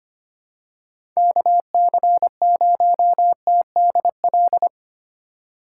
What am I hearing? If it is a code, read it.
Morse code, heard as a keyed tone, about 25 wpm, 700 Hz. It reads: KC0TDL
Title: USA Call Signs - 6 Characters - 2x Character Spacing - 25wpm